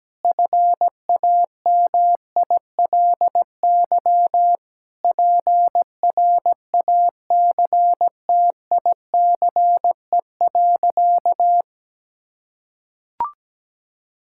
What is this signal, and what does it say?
Morse code, heard as a keyed tone, about 17 wpm, 700 Hz. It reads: FAMILY PRACTICE.
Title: Sentences from Top 1000 Words - Review Part 5 - 17wpm